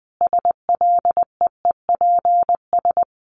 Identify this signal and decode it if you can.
Morse code, heard as a keyed tone, about 20 wpm, 700 Hz. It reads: SLEEPS
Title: Additional Easy Phrases 20wpm